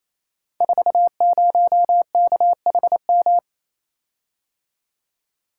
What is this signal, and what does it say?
Morse code, heard as a keyed tone, about 28 wpm, 700 Hz. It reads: 40KHM
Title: Code Groups - 5 Letter:Numbers - 28wpm